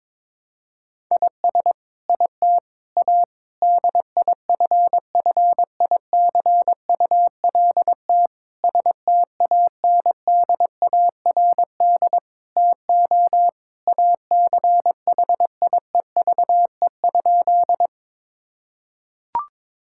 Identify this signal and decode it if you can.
Morse code, heard as a keyed tone, about 22 wpm, 700 Hz. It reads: IS IT A DIFFICULT STANDARD TO ACHIEVE?
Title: Sentences from Top 1000 Words - Review Part 6 - 22wpm